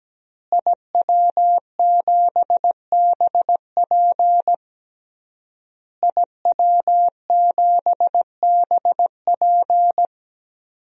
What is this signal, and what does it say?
Morse code, heard as a keyed tone, about 17 wpm, 700 Hz. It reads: IW7BP IW7BP
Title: DXCC-100 Call Signs - 5 Characters - Repeated 3x - 17wpm